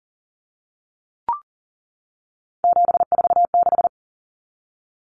Morse code, heard as a keyed tone, about 40 wpm, 700 Hz. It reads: 746
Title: Common Cut Numbers - 3-Digit 40wpm